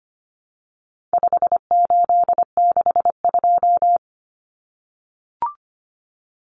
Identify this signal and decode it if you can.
Morse code, heard as a keyed tone, about 25 wpm, 700 Hz. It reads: 5862
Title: Common Cut Numbers - 4-Digit 25wpm